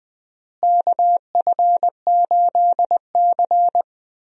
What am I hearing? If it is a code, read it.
Morse code, heard as a keyed tone, about 20 wpm, 700 Hz. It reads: KF8C
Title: USA Call Signs - 4 Characters - Repeated 3x - 20wpm